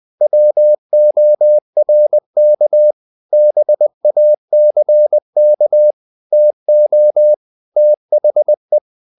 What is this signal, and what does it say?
Morse code, heard as a keyed tone, about 20 wpm, 600 Hz. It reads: WORK BACK TO THE
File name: the old man and the sea / the old man and the sea part 1 20wpm